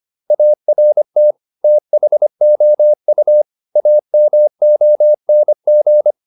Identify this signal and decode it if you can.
Morse code, heard as a keyed tone, about 25 wpm, 600 Hz. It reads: ART THOU AMONG
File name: the old man and the sea part 1 25wpm